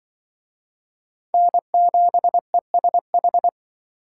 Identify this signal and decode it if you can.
Morse code, heard as a keyed tone, about 23 wpm, 700 Hz. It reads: N7ESH
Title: Speed Racing - USA Call Signs 20wpm